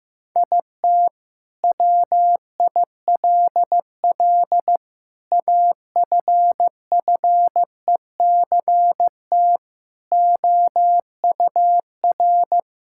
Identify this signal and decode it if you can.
Morse code, heard as a keyed tone, about 15 wpm, 700 Hz. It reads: IT WILL AFFECT OUR
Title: Sentences from Top 1000 Words - Review Part 6 - 15wpm